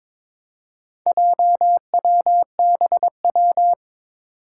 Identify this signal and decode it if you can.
Morse code, heard as a keyed tone, about 22 wpm, 700 Hz. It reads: JWBW
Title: Four Challenging Letter-Number Combinations 22wpm